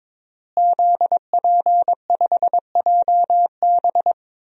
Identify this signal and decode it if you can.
Morse code, heard as a keyed tone, about 22 wpm, 700 Hz. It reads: ZP5JB
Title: DXCC-100 Call Signs - 5 Characters - Repeated 2x - 22wpm